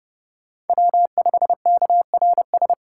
Morse code, heard as a keyed tone, about 30 wpm, 700 Hz. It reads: W5KRS
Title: USA Call Signs - 5 Characters - 3x Character Spacing - 30wpm